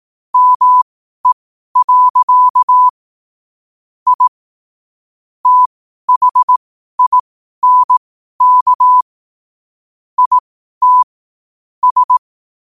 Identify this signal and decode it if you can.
Morse code, heard as a keyed tone, about 18 wpm, spaced as for 12 wpm, 1000 Hz. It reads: ME. I THINK IT S
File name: qotdfiles / 12wpm